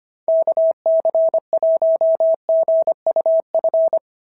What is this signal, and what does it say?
Morse code, heard as a keyed tone, about 25 wpm, 690 Hz. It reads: KC1GUF
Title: SOTA QSO Practice 25wpm